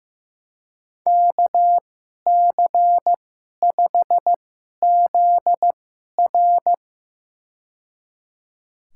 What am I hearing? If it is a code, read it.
Morse code, heard as a keyed tone, about 15 wpm, spaced as for 11 wpm, 700 Hz. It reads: KC5ZR
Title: USA Call Signs - 5 Characters - 2x Character Spacing - 15wpm